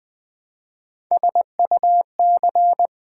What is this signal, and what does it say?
Morse code, heard as a keyed tone, about 20 wpm, 700 Hz. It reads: SUC
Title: Build A Word - 1 Letter At A Time - 4-Letter Words - 20wpm